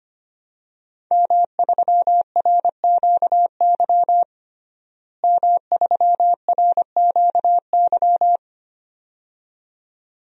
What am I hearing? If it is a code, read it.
Morse code, heard as a keyed tone, about 25 wpm, 700 Hz. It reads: M3RQY M3RQY
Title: DXCC-100 Call Signs - 5 Characters - Repeated 2x - 25wpm